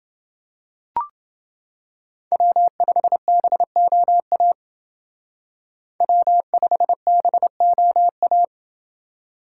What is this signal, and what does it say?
Morse code, heard as a keyed tone, about 28 wpm, 700 Hz. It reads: W5BOA W5BOA